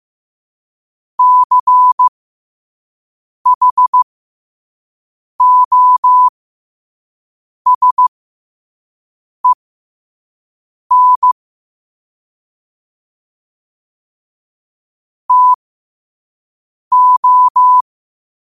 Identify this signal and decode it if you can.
Morse code, heard as a keyed tone, about 15 wpm, spaced as for 5 wpm, 1000 Hz. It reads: CHOSEN TO
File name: qotdfiles / 5wpm